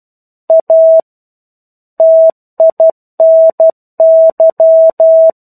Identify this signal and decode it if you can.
Morse code, heard as a keyed tone, about 12 wpm, 650 Hz. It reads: A TINY
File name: 01-Chapter-01 12 WPM